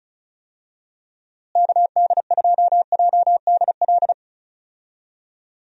Morse code, heard as a keyed tone, about 36 wpm, 700 Hz. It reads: KD2JDL
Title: USA Call Signs - 6 Characters - Repeated 3x - 35wpm